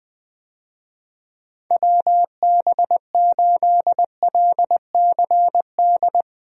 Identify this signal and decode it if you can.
Morse code, heard as a keyed tone, about 21 wpm, 700 Hz. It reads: WB8LCD
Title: USA Call Signs - 6 Characters - Repeated 2x - 20wpm